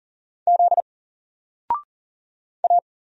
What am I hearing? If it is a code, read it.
Morse code, heard as a keyed tone, about 40 wpm, 700 Hz. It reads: Z A
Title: Single Letter-Number 40wpm